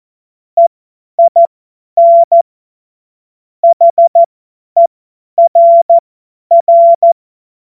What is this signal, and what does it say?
Morse code, heard as a keyed tone, about 14 wpm, spaced as for 10 wpm, 680 Hz. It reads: EIN HERR
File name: Morsekurs / Witze / Witz 09 April 2024 50 F70